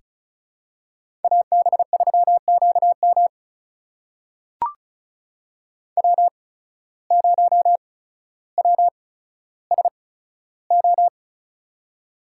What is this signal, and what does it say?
Morse code, heard as a keyed tone, about 34 wpm, spaced as for 15 wpm, 700 Hz. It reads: AB3QM W0WSO
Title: USA Call Signs - 5 Characters - 8x Character Spacing - 35wpm